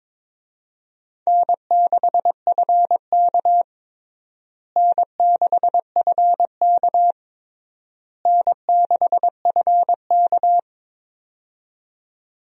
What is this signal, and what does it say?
Morse code, heard as a keyed tone, about 22 wpm, 700 Hz. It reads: N6FK N6FK N6FK
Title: USA Call Signs - 4 Characters - Repeated 3x - 22wpm